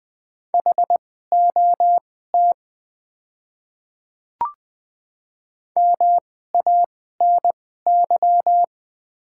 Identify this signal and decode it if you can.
Morse code, heard as a keyed tone, about 20 wpm, spaced as for 15 wpm, 700 Hz. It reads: HOT MANY
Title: 2x Character Spacing - Top 100 Words - 20wpm